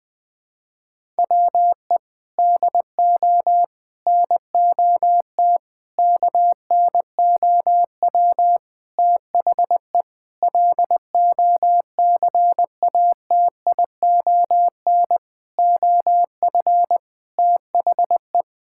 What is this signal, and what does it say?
Morse code, heard as a keyed tone, about 20 wpm, 700 Hz. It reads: WE DO NOT KNOW THE LOCATION OF THE
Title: Sentences from Top 1000 Words - Review Part 6 - 20wpm